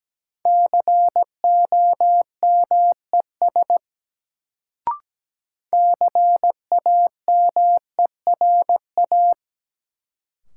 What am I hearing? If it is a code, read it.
Morse code, heard as a keyed tone, about 17 wpm, 700 Hz. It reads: COMES CAMERA
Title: Sentences from Top 1000 Words - Review Part 5 - New Word Drill - 17wpm